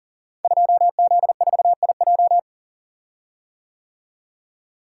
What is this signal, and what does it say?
Morse code, heard as a keyed tone, about 39 wpm, 700 Hz. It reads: 2Z4IJ
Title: Code Groups - 5 Letter:Numbers - 40wpm